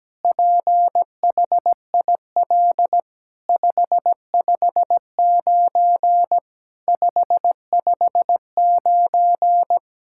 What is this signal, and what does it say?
Morse code, heard as a keyed tone, about 17 wpm, 700 Hz. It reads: PHIL 559 559
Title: SOTA QSO Practice 17wpm